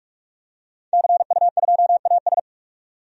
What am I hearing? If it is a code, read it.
Morse code, heard as a keyed tone, about 42 wpm, 700 Hz. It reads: CU2AS
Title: DXCC-100 Call Signs - 5 Characters - 45wpm